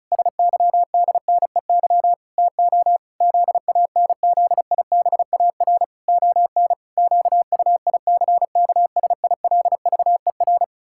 Morse code, heard as a keyed tone, about 36 wpm, 700 Hz. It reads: SYDNEY TO ZANZIBAR ON QUICKSILVER
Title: Pangram Challenge v2 - 35wpm